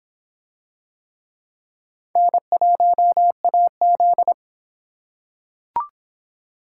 Morse code, heard as a keyed tone, about 24 wpm, 700 Hz. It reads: N1AZ